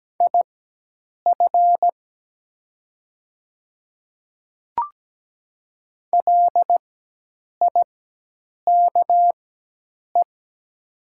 Morse code, heard as a keyed tone, about 17 wpm, spaced as for 8 wpm, 700 Hz. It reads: IF LIKE
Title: 4x Character Spacing - Top 100 Words - 17wpm